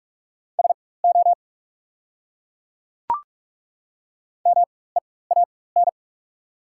Morse code, heard as a keyed tone, about 42 wpm, spaced as for 20 wpm, 690 Hz. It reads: SO MEAN
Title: 4x Character Spacing - Top 200 Words - 45wpm